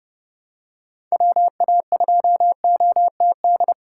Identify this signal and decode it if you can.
Morse code, heard as a keyed tone, about 30 wpm, 700 Hz. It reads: WA2OTD
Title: USA Call Signs - 6 Characters - 2x Character Spacing - 30wpm